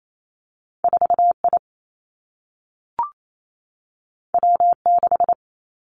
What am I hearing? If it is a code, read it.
Morse code, heard as a keyed tone, about 28 wpm, 700 Hz. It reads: 4 I W 6
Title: Two Challenging Letter-Number Combinations 28wpm